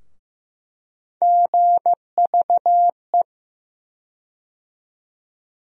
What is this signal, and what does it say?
Morse code, heard as a keyed tone, about 15 wpm, 700 Hz. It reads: GVE
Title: USA Call Sign Postfixes - 3 Letters 15wpm